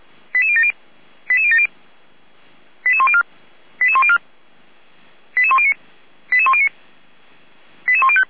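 Five-tone selective call (ZVEI), heard as quick runs of five tones, tones of about 70 ms. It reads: 88070 88070 88104 88104 88109 88109 88105
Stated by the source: ZVEI five-tone selective call received with an SDR.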